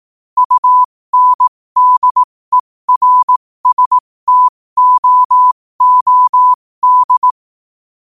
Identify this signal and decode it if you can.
Morse code, heard as a keyed tone, about 18 wpm, spaced as for 15 wpm, 1000 Hz. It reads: UNDERSTOOD
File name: qotdfiles / 15wpm